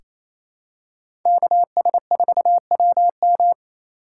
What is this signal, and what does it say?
Morse code, heard as a keyed tone, about 28 wpm, 700 Hz. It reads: KS4WM
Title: USA Call Signs - 5 Characters - 2x Character Spacing - 28wpm